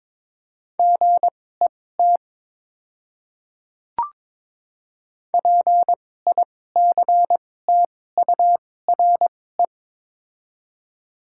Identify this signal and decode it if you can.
Morse code, heard as a keyed tone, about 22 wpm, spaced as for 16 wpm, 700 Hz. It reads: GET PICTURE
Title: 2x Character Spacing - Top 200 Words - 22wpm